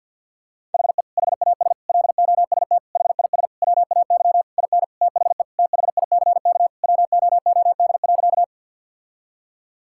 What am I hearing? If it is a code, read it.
Morse code, heard as a keyed tone, about 50 wpm, 700 Hz. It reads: HE HAS LOST HIS WAY IN THE THICK WOOD.
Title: Sentences from Top 1600 Words 50wpm